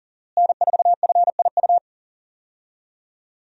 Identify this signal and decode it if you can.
Morse code, heard as a keyed tone, about 40 wpm, 700 Hz. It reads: N4FIU
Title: USA Call Signs - 5 Characters - Repeated 2x - 40wpm